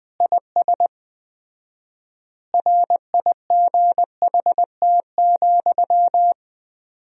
Morse code, heard as a keyed tone, about 20 wpm, 700 Hz. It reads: IS RIGHT,
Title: 4x Word-Spacing - Sentences from Top 200 Words - 20wpm